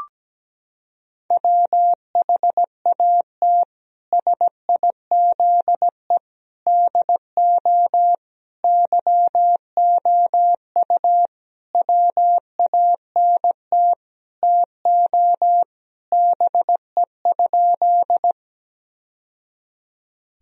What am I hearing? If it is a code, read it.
Morse code, heard as a keyed tone, about 17 wpm, 700 Hz. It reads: WHAT SIZE DO YOU WANT TO BE?
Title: Sentences from Top 1000 Words - Review Part 2 - 17wpm